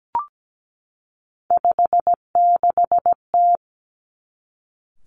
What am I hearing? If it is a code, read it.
Morse code, heard as a keyed tone, about 17 wpm, 700 Hz. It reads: 56T
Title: Three Challenging Letter-Number Combinations 17wpm